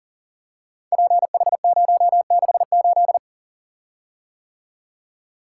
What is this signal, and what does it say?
Morse code, heard as a keyed tone, about 42 wpm, 700 Hz. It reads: PH068
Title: Code Groups - 5 Letter:Numbers - 40wpm